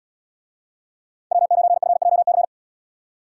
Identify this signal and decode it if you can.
Morse code, heard as a keyed tone, about 76 wpm, 700 Hz. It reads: W9FJC